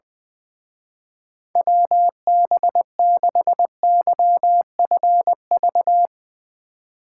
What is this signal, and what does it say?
Morse code, heard as a keyed tone, about 19 wpm, 700 Hz. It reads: WB6YFV
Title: Speed Racing - USA 2x3 Call Signs 20wpm